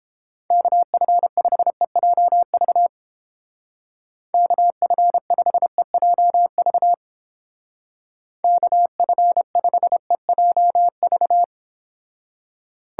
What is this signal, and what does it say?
Morse code, heard as a keyed tone, about 29 wpm, 700 Hz. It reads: KF5EJV KF5EJV KF5EJV